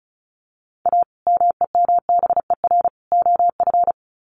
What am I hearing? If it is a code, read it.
Morse code, heard as a keyed tone, about 34 wpm, 700 Hz. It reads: A MEMBER OF
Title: Sets of 3 Words 35wpm